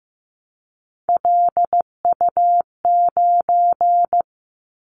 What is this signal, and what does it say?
Morse code, heard as a keyed tone, about 15 wpm, 700 Hz. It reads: LU9
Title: Callsign Prefixes for Top 50 DXCC - 15wpm